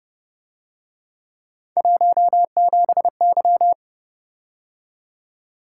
Morse code, heard as a keyed tone, about 28 wpm, 700 Hz. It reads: 17Y